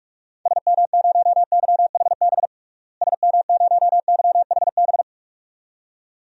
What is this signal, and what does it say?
Morse code, heard as a keyed tone, about 41 wpm, 700 Hz. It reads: SM0YHB SM0YHB
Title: DXCC-100 Call Signs - 6 Characters - Repeated 2x - 45wpm